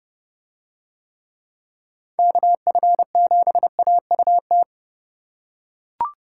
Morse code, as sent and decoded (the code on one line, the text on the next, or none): -.- ..-. --... .- ..- -
KF7AUT